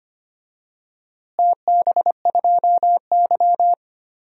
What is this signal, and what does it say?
Morse code, heard as a keyed tone, about 25 wpm, 700 Hz. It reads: TB2Y
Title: DXCC-100 Call Signs - 4 Characters - Repeated 2x - 25wpm